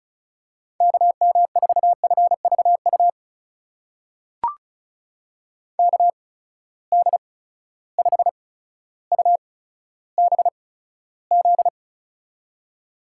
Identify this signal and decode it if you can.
Morse code, heard as a keyed tone, about 36 wpm, spaced as for 15 wpm, 700 Hz. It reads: KM4FVU KD5UBZ
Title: USA Call Signs - 6 Characters - 8x Character Spacing - 35wpm